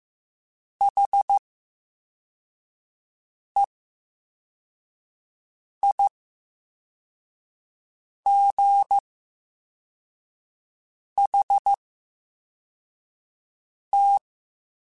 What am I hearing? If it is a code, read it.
Morse code, heard as a keyed tone, about 15 wpm, 800 Hz. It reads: H E I G H T
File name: Special INGs-mp3s / Special INGs-mp3s / VerySlowToFaster Ing @13